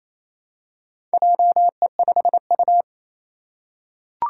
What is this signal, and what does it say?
Morse code, heard as a keyed tone, about 29 wpm, 700 Hz. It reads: JE5U
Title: Four Challenging Letter-Number Combinations 28wpm